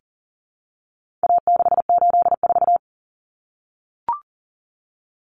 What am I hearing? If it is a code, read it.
Morse code, heard as a keyed tone, about 40 wpm, 700 Hz. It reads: A684
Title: Common Cut Numbers - 4-Digit 40wpm